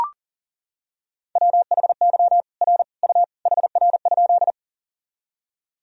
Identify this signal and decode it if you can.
Morse code, heard as a keyed tone, about 40 wpm, 700 Hz. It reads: WHY R U HR?
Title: Sentences from Top 200 Words and Abbreviations 40wpm